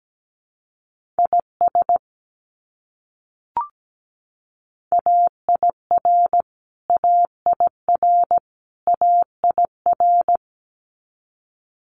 Morse code, heard as a keyed tone, about 17 wpm, 700 Hz. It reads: IS AIR AIR AIR
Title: Top 300 Words Repeated 17wpm